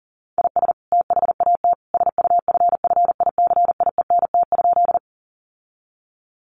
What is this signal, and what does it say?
Morse code, heard as a keyed tone, about 40 wpm, 700 Hz. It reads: IS THAT SUFFICIENT?
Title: Sentences from Top 1400 Words 40wpm